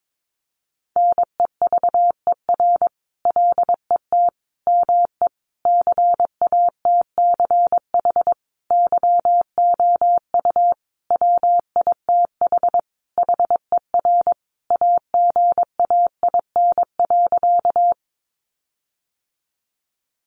Morse code, heard as a keyed tone, about 22 wpm, 700 Hz. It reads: NEVER LET ME CATCH YOU WITH HER AGAIN.
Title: Sentences from Top 1500 Words 22wpm